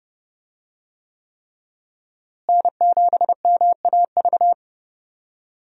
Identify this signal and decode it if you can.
Morse code, heard as a keyed tone, about 30 wpm, 700 Hz. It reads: N7MAV
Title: USA Call Signs - 5 Characters - Repeated 3x - 30wpm